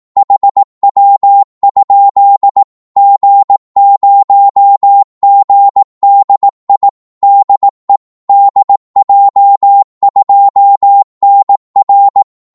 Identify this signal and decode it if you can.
Morse code, heard as a keyed tone, about 18 wpm, 800 Hz. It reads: HW? G0GDI DE DJ2NR
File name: Morse / qso-18wpm